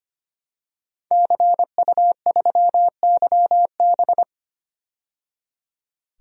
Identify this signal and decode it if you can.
Morse code, heard as a keyed tone, about 25 wpm, 700 Hz. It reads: CU3YB
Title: DXCC-100 Call Signs - 5 Characters - 25wpm